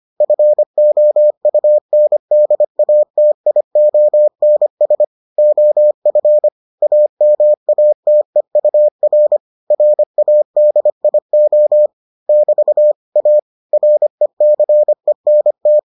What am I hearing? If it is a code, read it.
Morse code, heard as a keyed tone, about 25 wpm, 600 Hz. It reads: FOUNDATIONS OF AMATEUR RADIO <BT> A RECENT